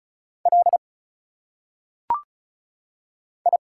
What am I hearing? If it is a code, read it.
Morse code, heard as a keyed tone, about 34 wpm, 700 Hz. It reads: L I
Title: Single Letters - Phonetic Pronunciation 35wpm